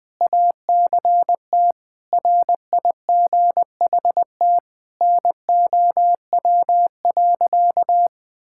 Morse code, heard as a keyed tone, about 20 wpm, 700 Hz. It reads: ACT RIGHT NOW.
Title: Sentences with 2-Word Phrases - Spoken Beginning - 20wpm